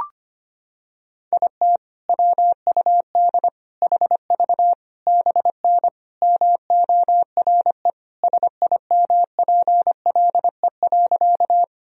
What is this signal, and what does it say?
Morse code, heard as a keyed tone, about 25 wpm, 700 Hz. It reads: IT WUD HV BN MORE SIMPLE.
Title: Sentences from Top 400 Words and Abbreviations 25wpm